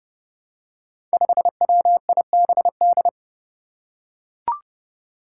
Morse code, heard as a keyed tone, about 30 wpm, 700 Hz. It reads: 5WIBD
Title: Five Challenging Letter-Number Combinations 30wpm